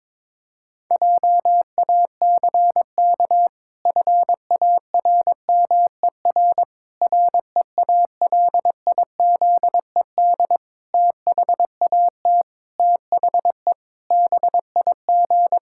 Morse code, heard as a keyed tone, about 22 wpm, 700 Hz. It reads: JACK FARMER REALIZED THAT THE BIG
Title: Pangram Challenge v2 - 22wpm